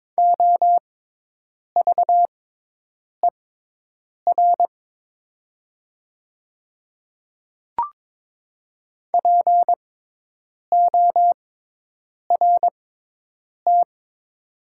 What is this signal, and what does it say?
Morse code, heard as a keyed tone, about 22 wpm, spaced as for 8 wpm, 700 Hz. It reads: OVER PORT
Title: 6x Character Spacing - Top 300 Words - 22wpm